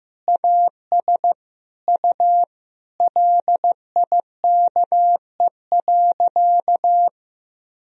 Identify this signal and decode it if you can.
Morse code, heard as a keyed tone, about 15 wpm, 700 Hz. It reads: AS U LIKE.
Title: Sentences from Top 500 Words and Abbreviations 15wpm